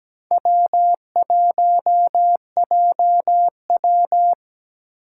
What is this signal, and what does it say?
Morse code, heard as a keyed tone, about 17 wpm, 700 Hz. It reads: W1JW
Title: USA Call Signs - 4 Characters - 6x Character Spacing - 17wpm